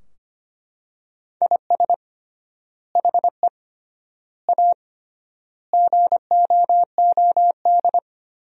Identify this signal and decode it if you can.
Morse code, heard as a keyed tone, about 25 wpm, 700 Hz. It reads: IS HE A GOOD
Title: 3x Word-Spacing - Sentences from Top 200 Words - 25wpm